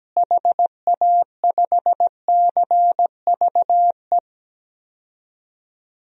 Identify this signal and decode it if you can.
Morse code, heard as a keyed tone, about 17 wpm, 700 Hz. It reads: HA5CVE
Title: DXCC-100 Call Signs - 6 Characters - Repeated 2x - 17wpm